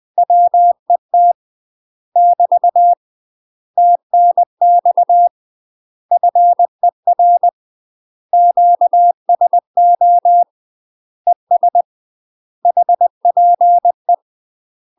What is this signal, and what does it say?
Morse code, heard as a keyed tone, about 20 wpm, 700 Hz. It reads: WET <BT> TNX FER QSO ES HPE
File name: UK MASTER QSOs TRAINING / 20WPM / UK MASTER QSOs 0005